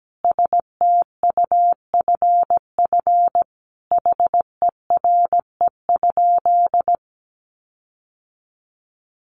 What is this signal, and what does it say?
Morse code, heard as a keyed tone, about 17 wpm, 700 Hz. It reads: STUFF HERE?
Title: Sentences from Top 1400 Words 17wpm